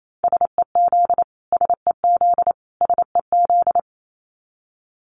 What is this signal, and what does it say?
Morse code, heard as a keyed tone, about 28 wpm, 700 Hz. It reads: SEZ SEZ SEZ
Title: Abbreviations Repeated 3x - 28wpm